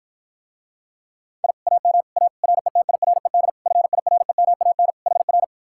Morse code, heard as a keyed tone, about 54 wpm, 700 Hz. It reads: I AM A RETIRED FIREMAN HR